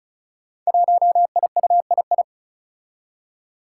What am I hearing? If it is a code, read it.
Morse code, heard as a keyed tone, about 36 wpm, 700 Hz. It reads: 1IUII
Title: Five Challenging Letter-Number Combinations 35wpm